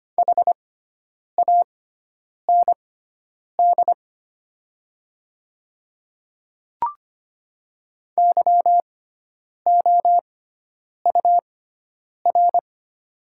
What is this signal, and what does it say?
Morse code, heard as a keyed tone, about 25 wpm, spaced as for 9 wpm, 700 Hz. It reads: HAND YOUR
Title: 6x Character Spacing - Top 300 Words - 25wpm